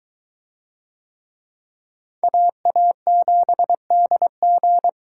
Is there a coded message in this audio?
Yes: Morse code from 2.2 to 4.9 s, heard as a keyed tone, about 21 wpm, 700 Hz. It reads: AA7DG